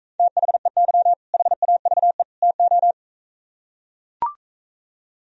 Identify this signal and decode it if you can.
Morse code, heard as a keyed tone, about 43 wpm, 700 Hz. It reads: THEY HAVE TO